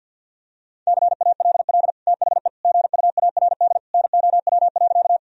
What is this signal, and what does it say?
Morse code, heard as a keyed tone, about 50 wpm, 700 Hz. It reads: CALL THE GUARD NOW.
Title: Sentences from Top 1000 Words - Review Part 4 - 50wpm